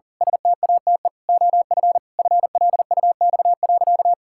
Morse code, heard as a keyed tone, about 40 wpm, 700 Hz. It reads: STATE OF FLUX.
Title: Pangram Challenge v2 - 40wpm